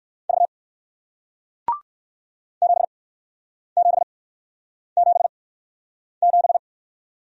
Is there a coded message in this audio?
Yes: Morse code from 0.3 to 6.6 s, heard as a keyed tone, about 59 wpm, spaced as for 9 wpm, 700 Hz. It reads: V 7777